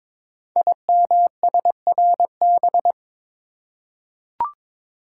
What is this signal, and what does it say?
Morse code, heard as a keyed tone, about 22 wpm, 700 Hz. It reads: IMSRB
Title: Code Groups - 5 Letters - 22wpm